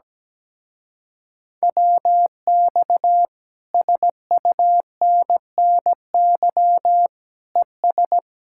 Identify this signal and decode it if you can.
Morse code, heard as a keyed tone, about 17 wpm, 700 Hz. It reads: WX SUNNY ES
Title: Speed-Racing Common QSO Phrases - 17wpm